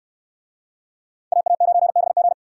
Morse code, heard as a keyed tone, about 64 wpm, 700 Hz. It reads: AA9BG